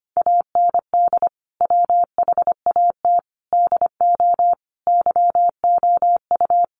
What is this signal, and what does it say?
Morse code, heard as a keyed tone, about 25 wpm, 700 Hz. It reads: AND WHAT DO YOU
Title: Sentences from Top 100 Words 25wpm